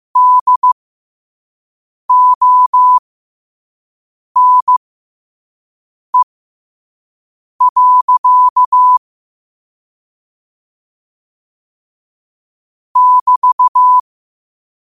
Morse code, heard as a keyed tone, about 15 wpm, spaced as for 5 wpm, 1000 Hz. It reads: DONE. <BT>
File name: qotdfiles / 5wpm